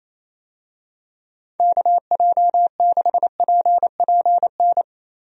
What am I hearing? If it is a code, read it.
Morse code, heard as a keyed tone, about 28 wpm, 700 Hz. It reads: KJ6PPN
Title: USA Call Signs - 6 Characters - Repeated 2x - 28wpm